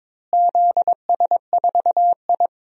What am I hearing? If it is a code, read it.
Morse code, heard as a keyed tone, about 22 wpm, 700 Hz. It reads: ZS4I
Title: DXCC-100 Call Signs - 4 Characters - Repeated 3x - 22wpm